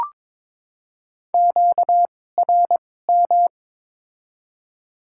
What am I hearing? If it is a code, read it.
Morse code, heard as a keyed tone, about 22 wpm, spaced as for 16 wpm, 700 Hz. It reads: QRM
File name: Top 90 QSO Elements - 2x Character Spacing - 22wpm